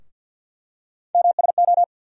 Morse code, heard as a keyed tone, about 50 wpm, 700 Hz. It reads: MSO
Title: USA Call Sign Postfixes - 3 Letters 50wpm